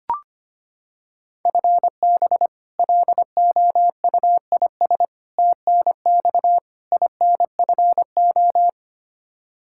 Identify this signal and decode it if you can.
Morse code, heard as a keyed tone, about 25 wpm, 700 Hz. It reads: FB LOUIS TNX INFO
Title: K1USN SST QSO Practice 25wpm